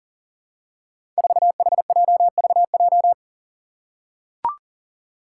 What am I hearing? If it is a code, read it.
Morse code, heard as a keyed tone, about 40 wpm, 700 Hz. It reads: 4HJVJ
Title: Five Challenging Letter-Number Combinations 40wpm